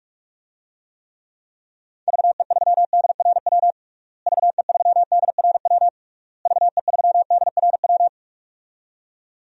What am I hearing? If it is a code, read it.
Morse code, heard as a keyed tone, about 46 wpm, 700 Hz. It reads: VE3DRW VE3DRW VE3DRW
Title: DXCC-100 Call Signs - 6 Characters - Repeated 3x - 45wpm